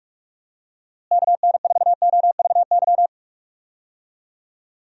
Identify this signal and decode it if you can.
Morse code, heard as a keyed tone, about 42 wpm, 700 Hz. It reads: KN4OVY
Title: USA Call Signs - 6 Characters - Repeated 3x - 45wpm